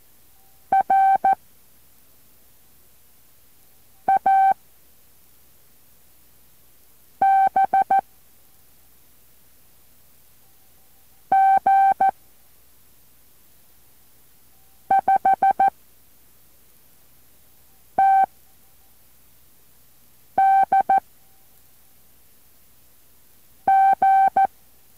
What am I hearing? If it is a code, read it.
Morse code, heard as a keyed tone, about 14 wpm, 780 Hz. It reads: R A B G 5 T D G